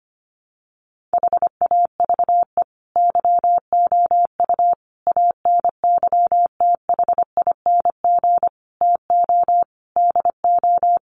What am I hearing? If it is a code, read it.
Morse code, heard as a keyed tone, about 25 wpm, 700 Hz. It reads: HAVE YOU ANYTHING TO DO
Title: Sentences from Top 1000 Words 25wpm